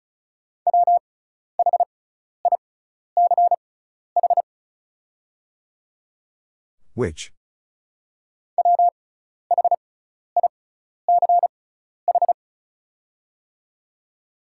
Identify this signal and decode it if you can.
Morse code, heard as a keyed tone, about 34 wpm, spaced as for 12 wpm, 700 Hz. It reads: WHICH WHICH
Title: 6x Character Spacing - Top 100 Words - 35wpm